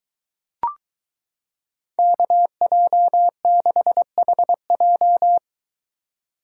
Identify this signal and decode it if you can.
Morse code, heard as a keyed tone, about 22 wpm, 700 Hz. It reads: KJ6HJ